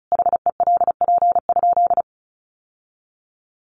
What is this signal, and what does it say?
Morse code, heard as a keyed tone, about 34 wpm, 700 Hz. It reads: HELP?
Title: Sentences from Top 1100 Words 35wpm